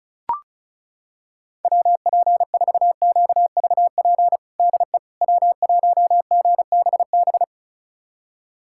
Morse code, heard as a keyed tone, about 34 wpm, 700 Hz. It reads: WP4QVP DE W1GBB
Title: USA Callsign DE Callsign 35wpm